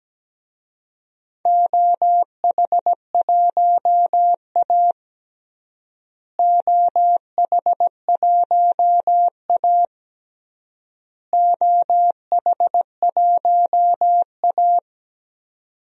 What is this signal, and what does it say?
Morse code, heard as a keyed tone, about 17 wpm, 700 Hz. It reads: OH1A OH1A OH1A
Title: DXCC-100 Call Signs - 4 Characters - Repeated 3x - 17wpm